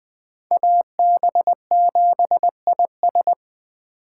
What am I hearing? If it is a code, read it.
Morse code, heard as a keyed tone, about 21 wpm, 700 Hz. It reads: AB7IS
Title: Speed Racing - USA Call Signs 17wpm